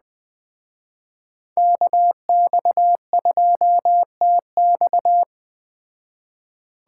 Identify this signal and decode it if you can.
Morse code, heard as a keyed tone, about 20 wpm, 700 Hz. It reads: KX2TX